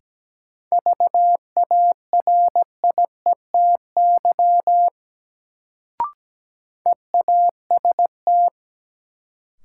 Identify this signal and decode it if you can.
Morse code, heard as a keyed tone, about 17 wpm, 700 Hz. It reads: VARIETY EAST
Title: Sets of 3 Words - New Word Drill - 17wpm